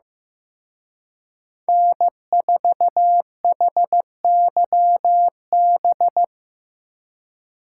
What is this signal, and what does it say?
Morse code, heard as a keyed tone, about 17 wpm, 700 Hz. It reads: N4HYB
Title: Speed Racing - USA 1x3 Call Signs 15wpm